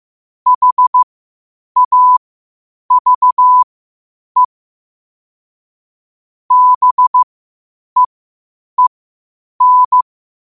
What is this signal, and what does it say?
Morse code, heard as a keyed tone, about 15 wpm, spaced as for 8 wpm, 1000 Hz. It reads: HAVE BEEN
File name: qotdfiles / 8wpm